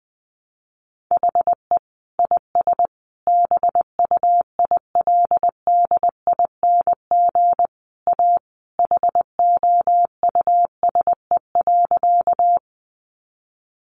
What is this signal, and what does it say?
Morse code, heard as a keyed tone, about 20 wpm, 700 Hz. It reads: HE IS BUILDING A HOUSE.
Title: Sentences from Top 700 Words 20wpm